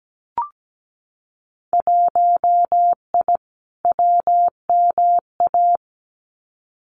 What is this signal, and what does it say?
Morse code, heard as a keyed tone, about 17 wpm, 700 Hz. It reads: 1I WMA
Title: Winter Field Day Exchanges Encore 17wpm